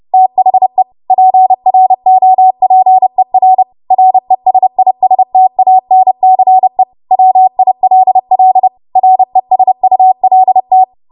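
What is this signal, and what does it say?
Morse code, heard as a keyed tone, about 30 wpm, 750 Hz. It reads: THE PROPER RESISTANCE WILL RESULT
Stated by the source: W1AW Web Code Practice Files, 30 WPM run from ARRL; text from April 2023 QST.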